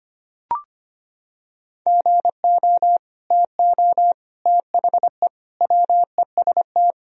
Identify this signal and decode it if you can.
Morse code, heard as a keyed tone, about 25 wpm, 700 Hz. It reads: GO TO THE WEST
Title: Sentences from Top 1000 Words - Review Part 2 - 25wpm